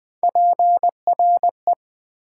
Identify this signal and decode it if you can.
Morse code, heard as a keyed tone, about 20 wpm, 700 Hz. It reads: PRE
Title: Prefixes and Suffixes 20wpm